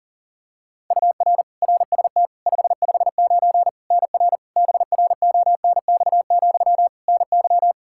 Morse code, heard as a keyed tone, about 40 wpm, 700 Hz. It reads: UR RST 559 NR BRONX, NY
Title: K1USN SST QSO Practice 40wpm